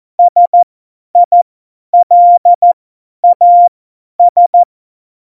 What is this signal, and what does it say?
Morse code, heard as a keyed tone, about 14 wpm, spaced as for 10 wpm, 700 Hz. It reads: SILAS
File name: Morsekurs / Witze / Witz 12. November 2019, 50 F70